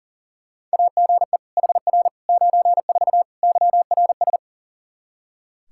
Morse code, heard as a keyed tone, about 40 wpm, 700 Hz. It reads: AGE HR 94 YRS
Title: K1USN SST QSO Practice 40wpm